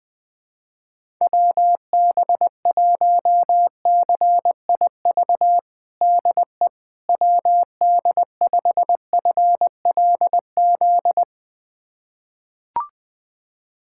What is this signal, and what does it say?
Morse code, heard as a keyed tone, about 20 wpm, 700 Hz. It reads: WB1CIV DE WD5FLZ
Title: USA Callsign DE Callsign - 6 Characters - 20wpm